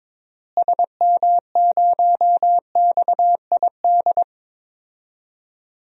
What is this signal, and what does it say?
Morse code, heard as a keyed tone, about 22 wpm, 700 Hz. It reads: SM0XID
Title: DXCC-100 Call Signs - 6 Characters - Repeated 2x - 22wpm